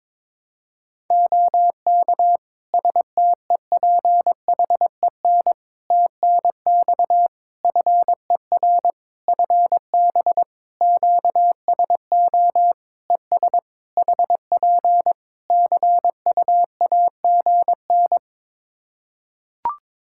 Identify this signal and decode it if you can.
Morse code, heard as a keyed tone, about 22 wpm, 700 Hz. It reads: OK STEPHEN TNX FER FB QSO ES HP CUAGN
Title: Practice QSO - Exchange 4 - Ending - 22wpm